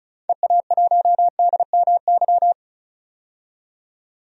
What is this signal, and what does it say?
Morse code, heard as a keyed tone, about 36 wpm, 700 Hz. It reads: EA1DMY
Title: DXCC-100 Call Signs - 6 Characters - 35wpm